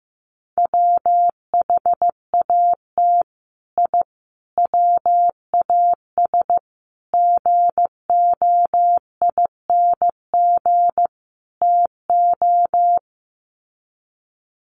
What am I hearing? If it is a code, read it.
Morse code, heard as a keyed tone, about 15 wpm, 700 Hz. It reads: WHAT I WAS GOING TO
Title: Sets of 5 Words Encore2 15wpm